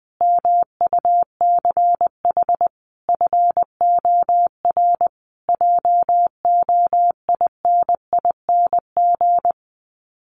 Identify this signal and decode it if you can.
Morse code, heard as a keyed tone, about 20 wpm, 700 Hz. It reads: MUCH FOR JOINING
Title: Sets of 5 Words Encore2 20wpm